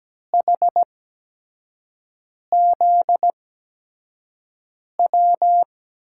Morse code, heard as a keyed tone, about 17 wpm, spaced as for 5 wpm, 700 Hz. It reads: HZW
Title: USA Call Signs - 6 Characters - 8x Character Spacing - 17wpm